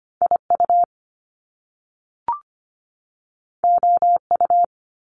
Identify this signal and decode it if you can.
Morse code, heard as a keyed tone, about 25 wpm, 700 Hz. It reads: IU OU
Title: Top Two Letter Combinations in Words 25wpm